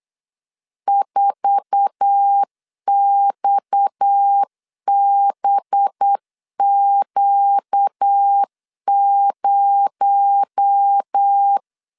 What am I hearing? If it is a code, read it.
Morse code, heard as a keyed tone, about 8 wpm, 800 Hz. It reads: 4XBQ0
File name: LEZIONI / 8 YJ45 / 8 YJ4 8wpm 3246